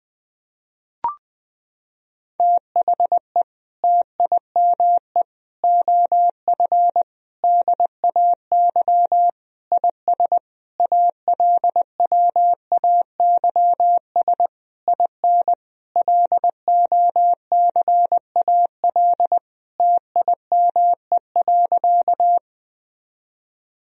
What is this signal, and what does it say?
Morse code, heard as a keyed tone, about 21 wpm, 700 Hz. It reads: THE TIME OF DAY IS ALWAYS IN LOCAL TIME.
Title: Sentences from Top 1000 Words - Review Part 3 - 20wpm